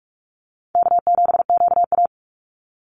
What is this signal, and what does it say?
Morse code, heard as a keyed tone, about 44 wpm, 700 Hz. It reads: K7QA
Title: Call Signs - 45wpm